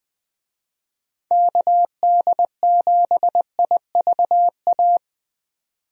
Morse code, heard as a keyed tone, about 20 wpm, 700 Hz. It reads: KD7IVA
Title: USA Call Signs - 6 Characters - 20wpm